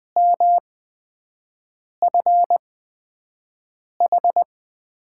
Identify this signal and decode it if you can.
Morse code, heard as a keyed tone, about 20 wpm, spaced as for 5 wpm, 700 Hz. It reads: MFH
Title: USA Call Signs - 8x Character Spacing - 20wpm